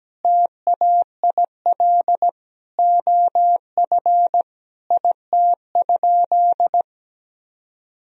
Morse code, heard as a keyed tone, about 17 wpm, 700 Hz. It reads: TAIL OF IT?
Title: Sentences from Top 1000 Words - Review Part 3 - 17wpm